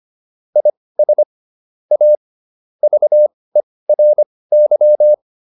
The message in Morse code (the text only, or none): IS A VERY